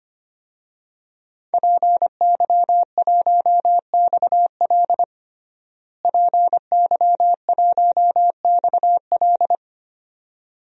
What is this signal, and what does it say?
Morse code, heard as a keyed tone, about 25 wpm, 700 Hz. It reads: PY1XL PY1XL
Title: DXCC-100 Call Signs - 5 Characters - Repeated 2x - 25wpm